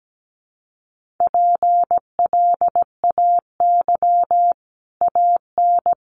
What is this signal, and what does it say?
Morse code, heard as a keyed tone, about 17 wpm, 700 Hz. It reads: PLAY AN
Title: Sets of 5 Words Encore 17wpm